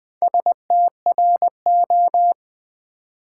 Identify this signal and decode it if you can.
Morse code, heard as a keyed tone, about 20 wpm, 700 Hz. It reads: STRO
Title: Build A Word - 4 Letters At A Time - 8-Letter Words - 20wpm